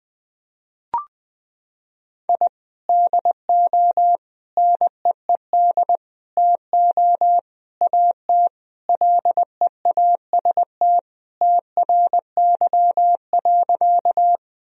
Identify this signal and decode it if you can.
Morse code, heard as a keyed tone, about 20 wpm, 700 Hz. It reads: I DO NEED TO AT LEAST TRY.
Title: Sentences from Top 1000 Words - Review Part 1 - 20wpm